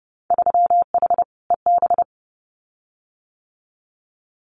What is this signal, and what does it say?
Morse code, heard as a keyed tone, about 30 wpm, 700 Hz. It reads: 3H EB
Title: Winter Field Day Exchanges Encore 30wpm